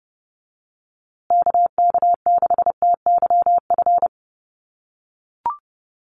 Morse code, heard as a keyed tone, about 30 wpm, 700 Hz. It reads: KK6TYF
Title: USA Call Signs Repeated 2x 30wpm